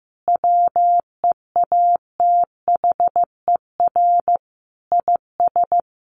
Morse code, heard as a keyed tone, about 15 wpm, 700 Hz. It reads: WEATHER IS
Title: Sentences from Top 1200 Words 15wpm